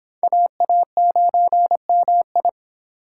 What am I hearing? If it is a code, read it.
Morse code, heard as a keyed tone, about 28 wpm, 700 Hz. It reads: AA9MI